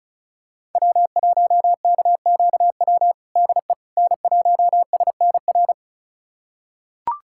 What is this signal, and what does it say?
Morse code, heard as a keyed tone, about 34 wpm, 700 Hz. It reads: W1KQW DE N1SNR
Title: USA Callsign DE Callsign - 5 Characters - 35wpm